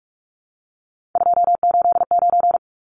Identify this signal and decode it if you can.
Morse code, heard as a keyed tone, about 44 wpm, 700 Hz. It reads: 289
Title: Numbers - 3-Digit 45wpm